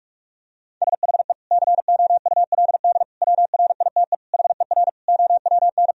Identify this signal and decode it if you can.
Morse code, heard as a keyed tone, about 41 wpm, 700 Hz. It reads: SHE COULD WRITE HER OWN
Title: Sentences from Top 1000 Words - Review Part 1 - 45wpm